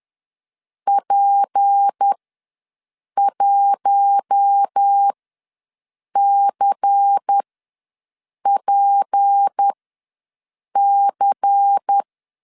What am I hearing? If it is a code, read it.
Morse code, heard as a keyed tone, about 10 wpm, spaced as for 6 wpm, 800 Hz. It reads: P1CPC
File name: LEZIONI / 6 PKX0 / 6 P ws 10wpm 3751